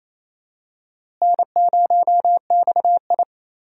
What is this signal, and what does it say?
Morse code, heard as a keyed tone, about 28 wpm, 700 Hz. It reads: N0XI
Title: USA Call Signs - 4 Characters - Repeated 3x - 28wpm